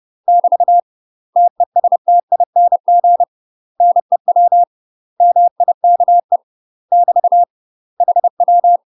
Morse code, heard as a keyed tone, about 30 wpm, 700 Hz. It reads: <BT> TESTING NEW MIKE <BT> HW
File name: UK MASTER QSOs TRAINING / 30WPM / UK MASTER QSOs 0015